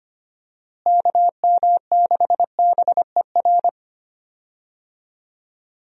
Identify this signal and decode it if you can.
Morse code, heard as a keyed tone, about 25 wpm, 700 Hz. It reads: KM6BER
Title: USA Call Signs - 6 Characters - Repeated 3x - 25wpm